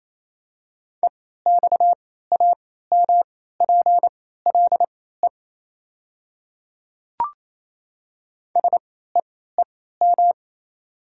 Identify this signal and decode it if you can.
Morse code, heard as a keyed tone, about 28 wpm, spaced as for 16 wpm, 700 Hz. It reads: EXAMPLE SEEM
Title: 3x Character Spacing - Top 300 Words - 28wpm